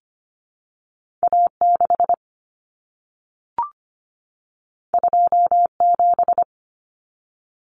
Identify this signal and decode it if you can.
Morse code, heard as a keyed tone, about 25 wpm, 700 Hz. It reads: A 6 2 7
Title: Common Cut Numbers - 2-Digit 25wpm